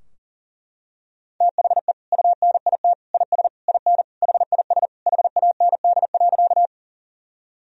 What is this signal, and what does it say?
Morse code, heard as a keyed tone, about 40 wpm, 700 Hz. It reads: THE UNIT IS IN HIS HAND.
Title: Sentences from Top 1000 Words - Review Part 2 - 40wpm